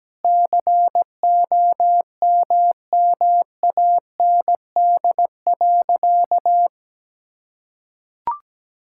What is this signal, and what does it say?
Morse code, heard as a keyed tone, about 17 wpm, 700 Hz. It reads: COMMAND.
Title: Sentences from Top 1600 Words 17wpm